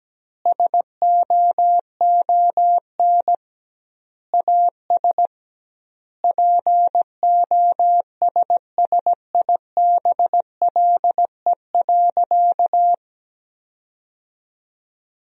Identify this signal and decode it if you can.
Morse code, heard as a keyed tone, about 17 wpm, 700 Hz. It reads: SOON AS POSSIBLE.
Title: 2x Word-Spacing - Sentences from Top 500 Words - 17wpm